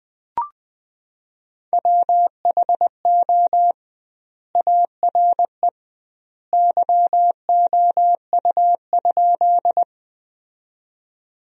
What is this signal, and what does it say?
Morse code, heard as a keyed tone, about 21 wpm, 700 Hz. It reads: WHO ARE YOU?
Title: 2x Word-Spacing - Sentences from Top 100 Words - 20wpm